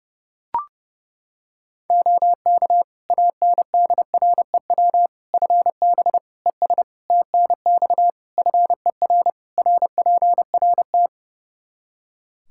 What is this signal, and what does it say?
Morse code, heard as a keyed tone, about 31 wpm, 700 Hz. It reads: OK ANDREW FB ES TNX FER RPRT
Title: Practice QSO - Exchange 2 - Info - 30wpm